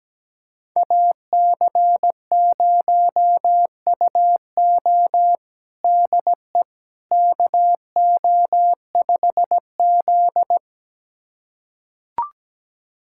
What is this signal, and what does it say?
Morse code, heard as a keyed tone, about 17 wpm, 700 Hz. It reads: AC0UO DE KO5Z
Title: USA Callsign DE Callsign 17wpm